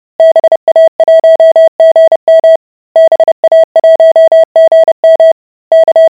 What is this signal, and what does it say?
Morse code, heard as a keyed tone, about 30 wpm, 650 Hz. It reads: BA1GM BA1GM K